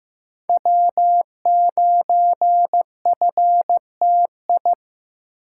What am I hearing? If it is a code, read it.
Morse code, heard as a keyed tone, about 15 wpm, 700 Hz. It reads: W9FTI
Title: USA Call Signs - 5 Characters - 8x Character Spacing - 15wpm